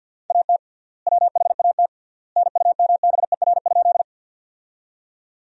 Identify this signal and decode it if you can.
Morse code, heard as a keyed tone, about 50 wpm, 700 Hz. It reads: AT WHAT NUMBER?
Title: 3x Word-Spacing - Sentences from Top 100 Words - 50wpm